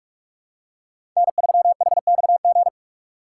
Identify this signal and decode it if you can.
Morse code, heard as a keyed tone, about 44 wpm, 690 Hz. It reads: N3HXG
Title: USA Call Signs - 5 Characters - 6x Character Spacing - 45wpm